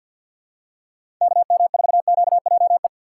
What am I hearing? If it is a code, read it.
Morse code, heard as a keyed tone, about 47 wpm, 700 Hz. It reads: KM4QJE
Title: USA Call Signs - 6 Characters - 6x Character Spacing - 50wpm